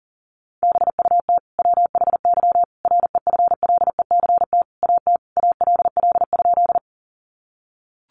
Morse code, heard as a keyed tone, about 40 wpm, 700 Hz. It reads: BUT WHY REFLECT AT ALL?
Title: Sentences from Top 1500 Words 40wpm